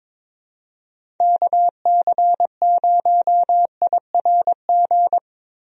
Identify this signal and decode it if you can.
Morse code, heard as a keyed tone, about 22 wpm, 700 Hz. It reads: KC0IRG
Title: USA Call Signs - 6 Characters - Repeated 2x - 22wpm